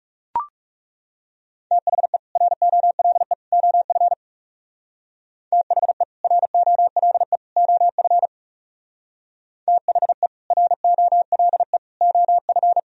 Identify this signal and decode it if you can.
Morse code, heard as a keyed tone, about 41 wpm, 700 Hz. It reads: THE ROLE OF THE ROLE OF THE ROLE OF